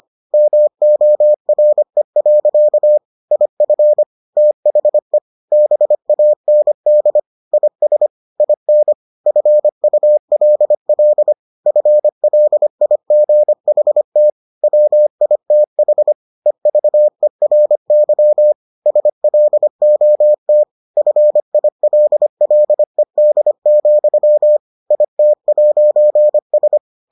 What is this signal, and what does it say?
Morse code, heard as a keyed tone, about 25 wpm, 600 Hz. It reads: MORE. IF THE BAND IS IN FULL FLIGHT WITH EVERY SLOT FILLED, IT'S